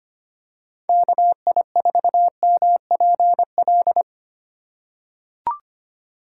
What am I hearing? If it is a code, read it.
Morse code, heard as a keyed tone, about 25 wpm, 700 Hz. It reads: KI4MPL
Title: USA Call Signs - 6 Characters - 3x Character Spacing - 25wpm